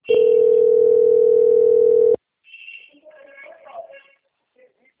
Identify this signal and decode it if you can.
Ringback tone (North America), 440 + 480 Hz, with speech.
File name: out- - - - - .3484